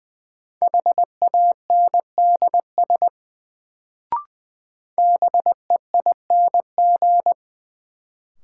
Morse code, heard as a keyed tone, about 20 wpm, 700 Hz. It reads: HANDS BEING
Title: Sets of 4 Words - New Word Drill - 20wpm